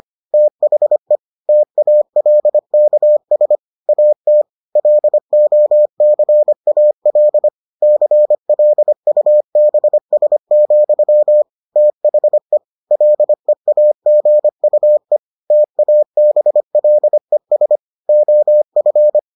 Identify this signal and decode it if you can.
Morse code, heard as a keyed tone, about 25 wpm, 600 Hz. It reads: THE TALKS AT LOCAL CLUBS, THE LEAGUE TABLES OF